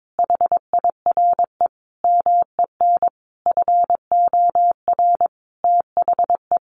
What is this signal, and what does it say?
Morse code, heard as a keyed tone, about 22 wpm, 700 Hz. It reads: HIRE MEN FOR THE
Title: Sentences from Top 1400 Words 22wpm